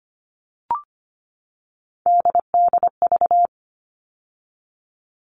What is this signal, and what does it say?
Morse code, heard as a keyed tone, about 25 wpm, 700 Hz. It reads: DDV
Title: All Cut Numbers - 3-Digit 25wpm